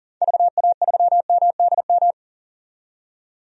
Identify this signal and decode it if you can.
Morse code, heard as a keyed tone, about 40 wpm, 700 Hz. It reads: VA3MDM
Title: DXCC-100 Call Signs - 6 Characters - 40wpm